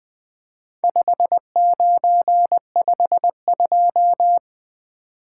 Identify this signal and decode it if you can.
Morse code, heard as a keyed tone, about 21 wpm, 700 Hz. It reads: 5952
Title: Four Letter-Number Combinations - 20wpm